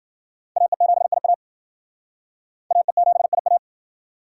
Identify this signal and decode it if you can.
Morse code, heard as a keyed tone, about 60 wpm, 700 Hz. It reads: AE7IA AE7IA